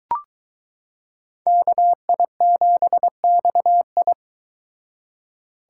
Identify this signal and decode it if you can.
Morse code, heard as a keyed tone, about 23 wpm, 700 Hz. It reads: KI7XI